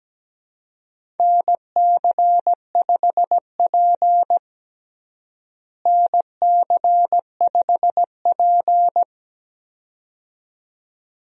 Morse code, heard as a keyed tone, about 17 wpm, 700 Hz. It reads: NC5P NC5P
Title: USA Call Signs - 4 Characters - Repeated 2x - 17wpm